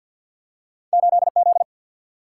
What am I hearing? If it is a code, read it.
Morse code, heard as a keyed tone, about 49 wpm, 700 Hz. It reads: 8 Z
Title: Single-Letter Single-Number 50wpm